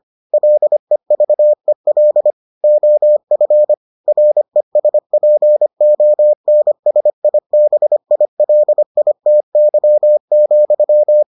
Morse code, heard as a keyed tone, about 25 wpm, 600 Hz. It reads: LEVEL OF RESPONSIBILITY,